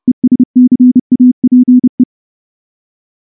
Morse code, heard as a keyed tone, about 30 wpm, 250 Hz. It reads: ESCAPE
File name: morse / morse-training-30WPM